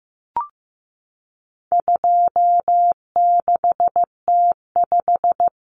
Morse code, heard as a keyed tone, about 15 wpm, 700 Hz. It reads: 26T5
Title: Common Cut Numbers - 4-Digit 15wpm